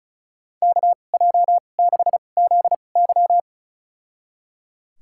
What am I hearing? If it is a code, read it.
Morse code, heard as a keyed tone, about 35 wpm, spaced as for 25 wpm, 700 Hz. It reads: KJ6ZY
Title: USA Call Signs - 5 Characters - 2x Character Spacing - 35wpm